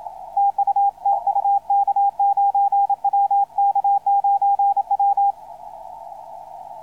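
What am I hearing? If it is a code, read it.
Morse code, heard as a keyed tone, about 27 wpm, 800 Hz, with radio noise. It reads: TU TU K9W K9W
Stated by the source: Morse on the 160 m band.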